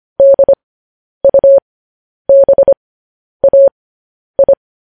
Morse code, heard as a keyed tone, about 25 wpm, spaced as for 10 wpm, 550 Hz. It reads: DUBAI